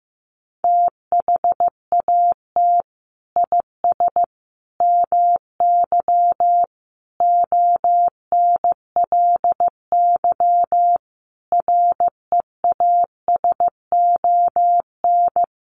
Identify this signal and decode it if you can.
Morse code, heard as a keyed tone, about 15 wpm, 700 Hz. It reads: THAT IS MY ONLY REASON
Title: Sentences from Top 700 Words 15wpm